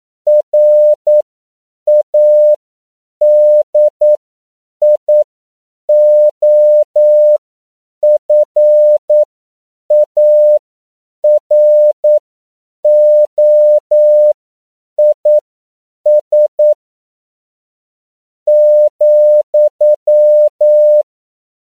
Morse code, heard as a keyed tone, about 9 wpm, spaced as for 7 wpm, 600 Hz. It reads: RADIOFAROIS ,
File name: CW / 10PPM